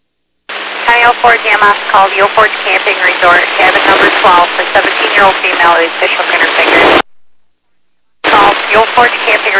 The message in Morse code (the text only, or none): none